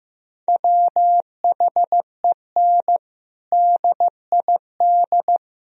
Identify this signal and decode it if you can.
Morse code, heard as a keyed tone, about 15 wpm, 700 Hz. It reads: WHEN DID
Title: Sentences from Top 1000 Words - Review Part 4 - 15wpm